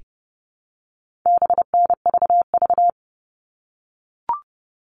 Morse code, heard as a keyed tone, about 30 wpm, 700 Hz. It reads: BNVV
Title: All Cut Numbers - 4-Digit 30wpm